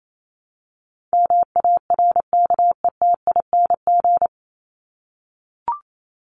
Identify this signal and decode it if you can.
Morse code, heard as a keyed tone, about 28 wpm, 700 Hz. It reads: MARKETING
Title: Top 751-1000 Words 28wpm